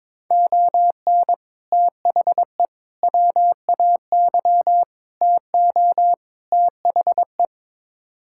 Morse code, heard as a keyed tone, about 22 wpm, 700 Hz. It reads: ON THE WAY TO THE
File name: practice / 22wpm / Build A Phrase - 5-Word Phrases - 22wpm